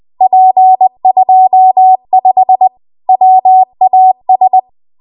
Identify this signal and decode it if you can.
Morse code, heard as a keyed tone, about 20 wpm, 750 Hz. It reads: P25 WAS